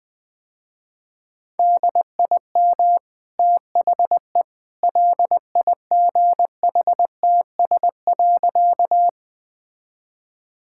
Morse code, heard as a keyed tone, about 20 wpm, 700 Hz. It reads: DIM THE LIGHTS.
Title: Easy Sentences with Hints - 20wpm